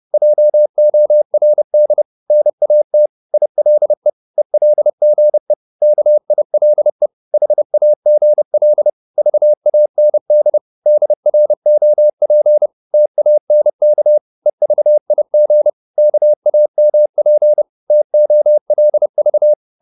Morse code, heard as a keyed tone, about 30 wpm, 600 Hz. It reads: JORD NAT ILE ELGE KILE HAGL VAND DROP TANK EVIG KAMP TOLV
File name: Korte ord DK 30wpm